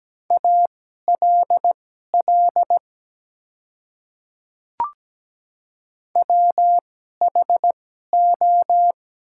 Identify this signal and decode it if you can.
Morse code, heard as a keyed tone, about 17 wpm, spaced as for 12 wpm, 700 Hz. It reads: ALL WHO
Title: 2x Character Spacing - Top 100 Words - 17wpm